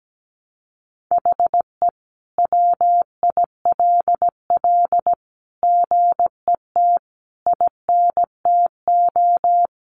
Morse code, heard as a keyed tone, about 17 wpm, 700 Hz. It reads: HE WILL GET INTO
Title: Sentences from Top 1500 Words 17wpm